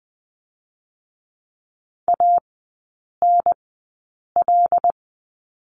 Morse code, heard as a keyed tone, about 20 wpm, spaced as for 8 wpm, 700 Hz. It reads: ANL
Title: 087 Lesson 15e - Number 5 - Three Characters 20wpm